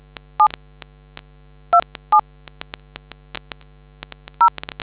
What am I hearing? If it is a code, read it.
DTMF, heard as telephone touch tones, tones of about 100 ms. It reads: 7 27 0